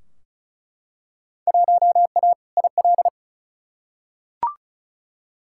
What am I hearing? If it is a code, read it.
Morse code, heard as a keyed tone, about 36 wpm, 700 Hz. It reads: 1A IL
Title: Field Day Exchanges Encore - 35wpm